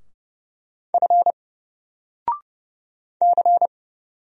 Morse code, heard as a keyed tone, about 30 wpm, 700 Hz. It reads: F C
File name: Single-Letter-Number-30wpm